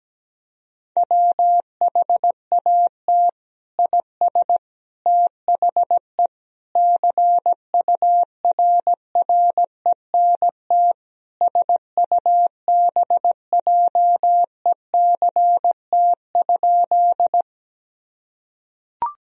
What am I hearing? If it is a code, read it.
Morse code, heard as a keyed tone, about 17 wpm, 700 Hz. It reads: WHAT IS THE CURRENT SUBJECT?
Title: Sentences from Top 1000 Words - Review Part 4 - 17wpm